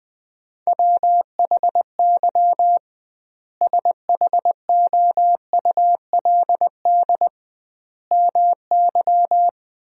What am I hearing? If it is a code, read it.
Morse code, heard as a keyed tone, about 21 wpm, 700 Hz. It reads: WHY SHOULD MY
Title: 2x Word-Spacing - Sentences from Top 500 Words - 20wpm